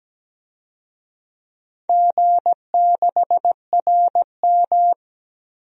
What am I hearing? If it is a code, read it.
Morse code, heard as a keyed tone, about 17 wpm, 700 Hz. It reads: G6RM
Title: DXCC-100 Call Signs - 4 Characters - 17wpm